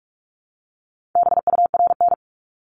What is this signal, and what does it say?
Morse code, heard as a keyed tone, about 46 wpm, 700 Hz. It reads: BURN
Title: Multi-letter Phonemes 45wpm